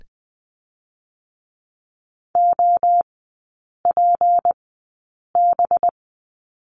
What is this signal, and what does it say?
Morse code, heard as a keyed tone, about 20 wpm, spaced as for 8 wpm, 690 Hz. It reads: OPB
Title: 191 Lesson 31e - Letter v - Three Characters 20wpm